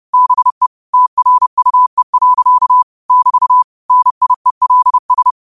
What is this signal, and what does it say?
Morse code, heard as a keyed tone, about 30 wpm, 1000 Hz. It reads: BE TRUE. <BT> NIELS
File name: qotdfiles / 30wpm